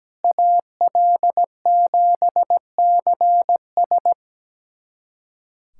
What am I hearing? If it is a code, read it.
Morse code, heard as a keyed tone, about 17 wpm, 700 Hz. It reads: AL7CS
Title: USA Callsign DE Callsign - 5 Characters - 17wpm